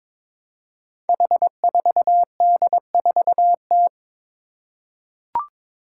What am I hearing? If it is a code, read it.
Morse code, heard as a keyed tone, about 22 wpm, 700 Hz. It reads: H4D4T
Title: Five Challenging Letter-Number Combinations 22wpm